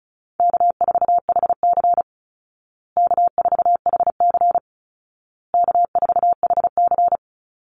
Morse code, heard as a keyed tone, about 34 wpm, 700 Hz. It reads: K4HC K4HC K4HC
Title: Easy Call Signs Repeated 3x - 35wpm